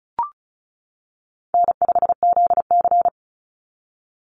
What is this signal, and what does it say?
Morse code, heard as a keyed tone, about 34 wpm, 700 Hz. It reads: N5ZC
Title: NA SOTA Chaser Callsigns 35wpm